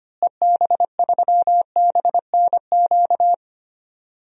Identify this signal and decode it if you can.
Morse code, heard as a keyed tone, about 25 wpm, 700 Hz. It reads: EB3BNQ
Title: DXCC-100 Call Signs - 6 Characters - Repeated 2x - 25wpm